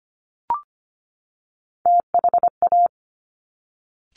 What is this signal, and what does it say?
Morse code, heard as a keyed tone, about 25 wpm, 700 Hz. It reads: THA
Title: Top Three Letter Combinations in Words 25wpm